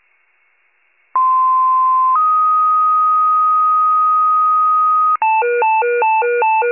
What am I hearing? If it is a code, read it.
Two-tone page, heard as a one-second tone, then a three-second tone, about 1040 then 1250 Hz. It is heard from 1.1 to 5.2 s.